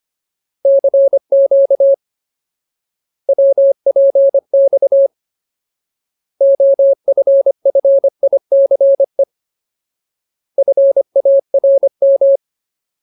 Morse code, heard as a keyed tone, about 25 wpm, 550 Hz. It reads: CQ WPX OFFICE FARM
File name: Ultimate Word List / 25WPM Ultimate